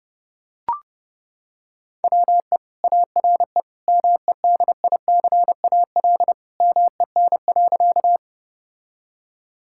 Morse code, heard as a keyed tone, about 30 wpm, 700 Hz. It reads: WE ARE MEDICAL MEN.
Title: Sentences from Top 1000 Words - Review Part 5 - 30wpm